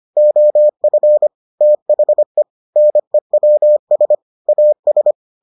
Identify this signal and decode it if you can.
Morse code, heard as a keyed tone, about 25 wpm, 600 Hz. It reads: OF THE NEWS AS